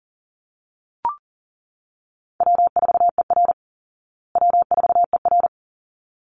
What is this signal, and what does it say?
Morse code, heard as a keyed tone, about 40 wpm, 700 Hz. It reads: W4ER W4ER
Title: Easy Call Signs Repeated 2x - 40wpm